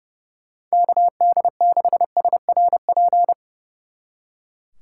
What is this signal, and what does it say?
Morse code, heard as a keyed tone, about 38 wpm, 700 Hz. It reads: KD6SRP